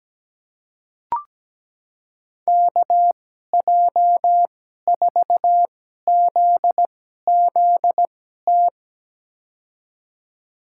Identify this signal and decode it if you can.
Morse code, heard as a keyed tone, about 17 wpm, spaced as for 12 wpm, 700 Hz. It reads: KJ4ZZT
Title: USA Call Signs - 2x Character Spacing - 17wpm